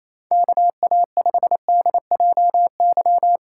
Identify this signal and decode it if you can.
Morse code, heard as a keyed tone, about 28 wpm, 700 Hz. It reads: KA5DJY
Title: USA Call Signs - 3x Character Spacing - 28wpm